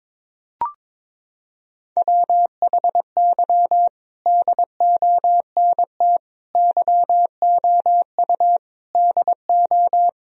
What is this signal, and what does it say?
Morse code, heard as a keyed tone, about 22 wpm, 700 Hz. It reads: WHY DONT YOU DO
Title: Sentences from Top 1000 Words - Review Part 4 - 22wpm